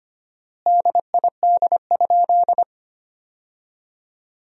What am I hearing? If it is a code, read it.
Morse code, heard as a keyed tone, about 25 wpm, 700 Hz. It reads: DID?
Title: 4x Word-Spacing - Sentences from Top 100 Words - 25wpm